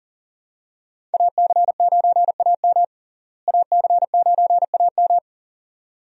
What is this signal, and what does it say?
Morse code, heard as a keyed tone, about 40 wpm, 700 Hz. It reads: AC9AM AC9AM
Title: USA Call Signs - 5 Characters - Repeated 2x - 40wpm